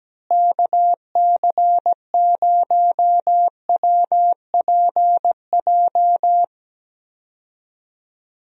Morse code, heard as a keyed tone, about 17 wpm, 700 Hz. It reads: KC0WPJ
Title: USA Call Signs - 6 Characters - Repeated 2x - 17wpm